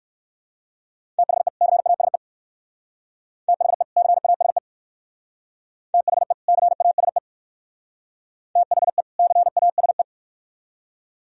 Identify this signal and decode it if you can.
Morse code, heard as a keyed tone, about 58 wpm, 700 Hz. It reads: THE CASE THE CASE THE CASE THE CASE